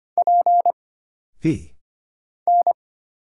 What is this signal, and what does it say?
Morse code, heard as a keyed tone, about 25 wpm, 700 Hz. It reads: P N
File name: Single Character (Letter:Number:Punctuation:Prosign) - Mind-Melt - 25wpm